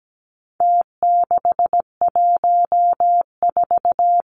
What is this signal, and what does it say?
Morse code, heard as a keyed tone, about 17 wpm, 700 Hz. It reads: T614
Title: Numbers - Contest Style - 4-Digit 17wpm